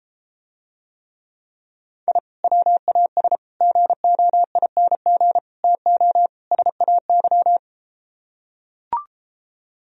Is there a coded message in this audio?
Yes: Morse code from 2.1 to 7.6 s, heard as a keyed tone, about 33 wpm, 700 Hz. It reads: I WAS GOING TO SAY